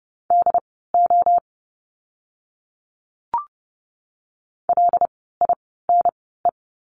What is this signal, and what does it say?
Morse code, heard as a keyed tone, about 30 wpm, spaced as for 17 wpm, 700 Hz. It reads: DO LINE
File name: course / 30:10wpm / 115 Lesson 19f - Letter w - Words 30wpm